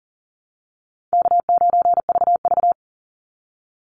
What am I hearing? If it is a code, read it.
Morse code, heard as a keyed tone, about 39 wpm, 700 Hz. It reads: K9VV
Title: Call Signs Repeated 3x - 40wpm